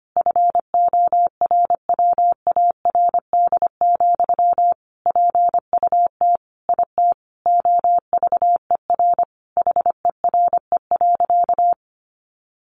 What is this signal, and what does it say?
Morse code, heard as a keyed tone, about 25 wpm, 700 Hz. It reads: FORWARD, PUT IT OVER HERE.
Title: Sentences from Top 1500 Words 25wpm